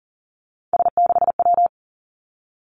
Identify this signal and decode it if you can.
Morse code, heard as a keyed tone, about 38 wpm, 700 Hz. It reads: S6W
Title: Three Challenging Letter-Number Combinations 40wpm